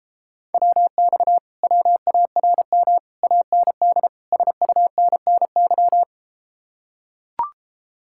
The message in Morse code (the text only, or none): WX WARM AND SUNNY